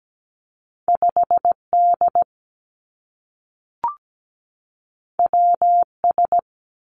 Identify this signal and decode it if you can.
Morse code, heard as a keyed tone, about 17 wpm, 700 Hz. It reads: 5D WS
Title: Two Challenging Letter-Number Combinations 17wpm